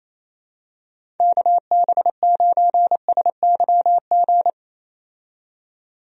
Morse code, heard as a keyed tone, about 28 wpm, 700 Hz. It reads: KB9SYG
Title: USA Call Signs - 6 Characters - 28wpm